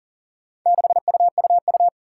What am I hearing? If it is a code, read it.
Morse code, heard as a keyed tone, about 41 wpm, 700 Hz. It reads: 6UUU
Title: Four Challenging Letter-Number Combinations 40wpm